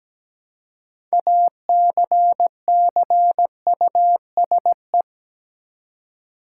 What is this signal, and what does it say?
Morse code, heard as a keyed tone, about 17 wpm, 700 Hz. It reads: ACCUSE
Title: Six Letter Words - 17wpm